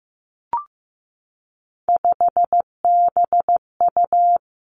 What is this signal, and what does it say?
Morse code, heard as a keyed tone, about 15 wpm, 700 Hz. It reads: 5BU
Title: Three Challenging Letter-Number Combinations 15wpm